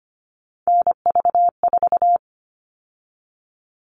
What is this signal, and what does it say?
Morse code, heard as a keyed tone, about 25 wpm, 700 Hz. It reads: NV4
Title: All Cut Numbers - 3-Digit 25wpm